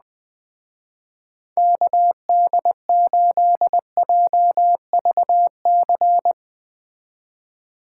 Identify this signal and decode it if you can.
Morse code, heard as a keyed tone, about 20 wpm, 700 Hz. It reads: KD8JVC
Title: Speed Racing - USA Call Signs 17wpm